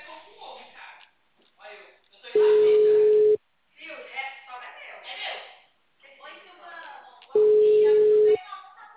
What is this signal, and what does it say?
Ringback tone (Europe, 425 Hz), with speech.